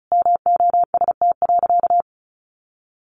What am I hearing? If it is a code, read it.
Morse code, heard as a keyed tone, about 34 wpm, 700 Hz. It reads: MOST.
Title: Sentences from Top 1400 Words 35wpm